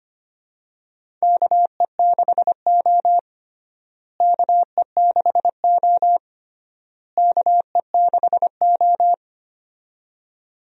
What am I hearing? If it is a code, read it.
Morse code, heard as a keyed tone, about 25 wpm, 700 Hz. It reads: KE6O KE6O KE6O
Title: USA Call Signs - 4 Characters - Repeated 3x - 25wpm